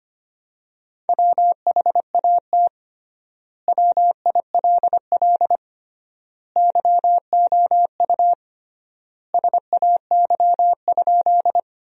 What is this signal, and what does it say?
Morse code, heard as a keyed tone, about 25 wpm, 700 Hz. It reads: WHAT WILL YOU SAY?
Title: 3x Word-Spacing - Sentences from Top 200 Words - 25wpm